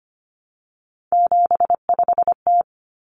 Morse code, heard as a keyed tone, about 25 wpm, 700 Hz. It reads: 75T
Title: Common Cut Numbers - 3-Digit 25wpm